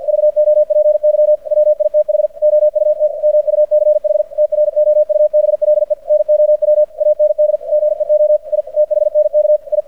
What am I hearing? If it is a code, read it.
Morse code, heard as a keyed tone, about 50 wpm, 600 Hz. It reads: VOPY WITH OUT MUCH TROUBLE NOW AND GEO I THNK I
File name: archived / CRSnet